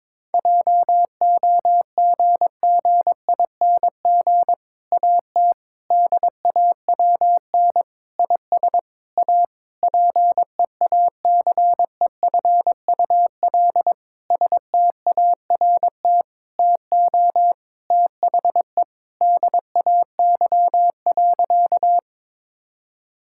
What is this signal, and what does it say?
Morse code, heard as a keyed tone, about 22 wpm, 700 Hz. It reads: JOGGING AT DAWN IS A PEACEFUL START TO THE DAY.
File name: Difficult Sentences with Hints - 22wpm